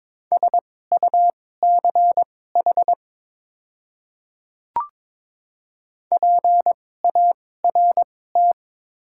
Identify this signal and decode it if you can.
Morse code, heard as a keyed tone, about 22 wpm, spaced as for 16 wpm, 700 Hz. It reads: SUCH PART
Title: 2x Character Spacing - Top 300 Words - 22wpm